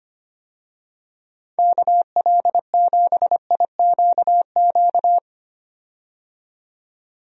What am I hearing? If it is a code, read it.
Morse code, heard as a keyed tone, about 25 wpm, 700 Hz. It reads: KL7IQQ
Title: DXCC-100 Call Signs - 6 Characters - 25wpm